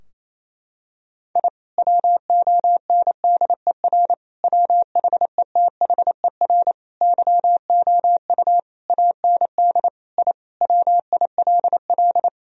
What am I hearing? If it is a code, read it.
Morse code, heard as a keyed tone, about 28 wpm, 700 Hz. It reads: I WONDER WHETHER YOU AND I WILL
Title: Sentences from Top 1000 Words - Review Part 3 - 28wpm